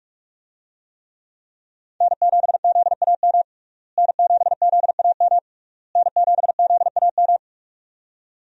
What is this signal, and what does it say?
Morse code, heard as a keyed tone, about 41 wpm, 700 Hz. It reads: N7ZAM N7ZAM N7ZAM
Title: USA Call Signs - 5 Characters - Repeated 3x - 45wpm